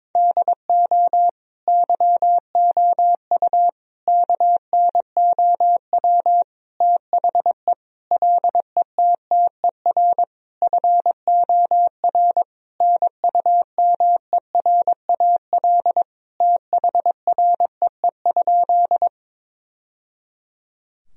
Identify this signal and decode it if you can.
Morse code, heard as a keyed tone, about 22 wpm, 700 Hz. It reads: DO YOU KNOW THE LETTER FOR NUMERAL THREE?
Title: Sentences from Top 1000 Words - Review Part 2 - 22wpm